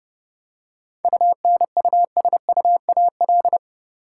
Morse code, heard as a keyed tone, about 30 wpm, 700 Hz. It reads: UNUSUAL
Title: Seven Letter Words - 30wpm